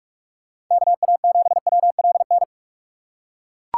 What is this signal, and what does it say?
Morse code, heard as a keyed tone, about 42 wpm, 690 Hz. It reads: KA7WLN
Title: USA Call Signs - 6 Characters - 2x Character Spacing - 45wpm